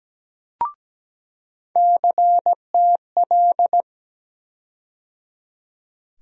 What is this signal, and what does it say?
Morse code, heard as a keyed tone, about 17 wpm, 700 Hz. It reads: CTL
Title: USA Call Sign Postfixes - 3 Letters 17wpm